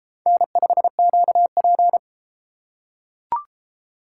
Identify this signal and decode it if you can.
Morse code, heard as a keyed tone, about 32 wpm, 700 Hz. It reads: N5QP